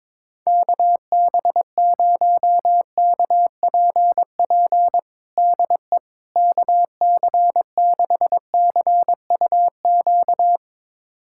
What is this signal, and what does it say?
Morse code, heard as a keyed tone, about 22 wpm, 700 Hz. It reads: KB0KPP DE KC6CUQ
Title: USA Callsign DE Callsign 22wpm